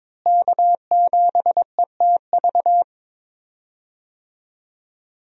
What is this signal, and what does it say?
Morse code, heard as a keyed tone, about 22 wpm, 700 Hz. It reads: K7ETV
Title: USA Call Signs - 5 Characters - Repeated 3x - 22wpm